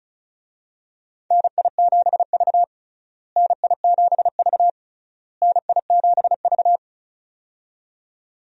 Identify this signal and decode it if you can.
Morse code, heard as a keyed tone, about 36 wpm, 700 Hz. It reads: NI7V NI7V NI7V
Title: USA Call Signs - 4 Characters - Repeated 3x - 35wpm